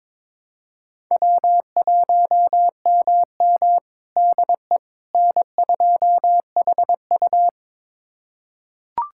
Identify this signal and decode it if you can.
Morse code, heard as a keyed tone, about 22 wpm, 700 Hz. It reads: W1MM DE N2HU
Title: USA Callsign DE Callsign - 4 Characters - 22wpm